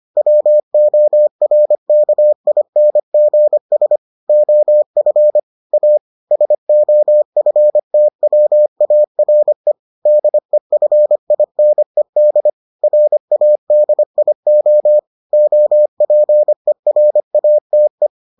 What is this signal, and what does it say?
Morse code, heard as a keyed tone, about 25 wpm, 600 Hz. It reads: WORKINGS OF A SOFTWARE DEFINED RADIO OPERATE